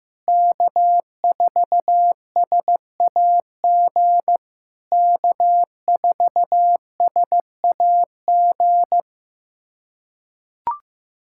Morse code, heard as a keyed tone, about 15 wpm, 700 Hz. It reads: K4SAG K4SAG
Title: SOTA QSO Practice 15wpm